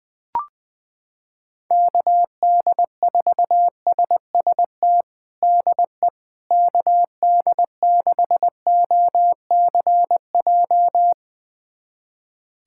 Morse code, heard as a keyed tone, about 20 wpm, 700 Hz. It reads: KD4SST DE KD6OCJ
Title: USA Callsign DE Callsign - 6 Characters - 20wpm